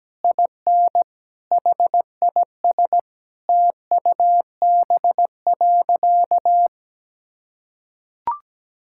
Morse code, heard as a keyed tone, about 17 wpm, 700 Hz. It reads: IN HIS TUB.
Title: Pangram Challenge v2 - 17wpm